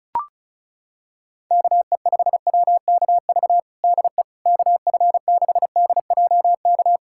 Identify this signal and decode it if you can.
Morse code, heard as a keyed tone, about 36 wpm, 700 Hz. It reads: KE5WKV DE KF6DJK
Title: USA Callsign DE Callsign - 6 Characters - 35wpm